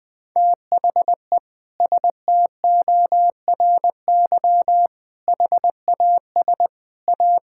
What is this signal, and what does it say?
Morse code, heard as a keyed tone, about 21 wpm, 700 Hz. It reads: THE STORY HAS A
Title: Sentences from Top 1000 Words - Review Part 6 - 20wpm